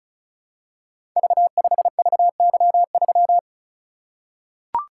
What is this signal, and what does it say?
Morse code, heard as a keyed tone, about 36 wpm, 700 Hz. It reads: V5VY3
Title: Code Groups - 5 Letter:Numbers - 35wpm